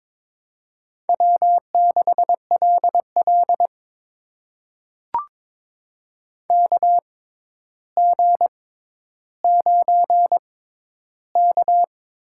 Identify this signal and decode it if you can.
Morse code, heard as a keyed tone, about 21 wpm, spaced as for 11 wpm, 700 Hz. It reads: W6LL KG9K
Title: USA Call Signs - 6x Character Spacing - 22wpm